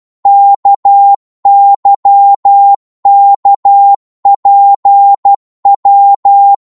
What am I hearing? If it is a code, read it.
Morse code, heard as a keyed tone, about 12 wpm, 800 Hz. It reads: KYKPW
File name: Morse / ltr-12wpm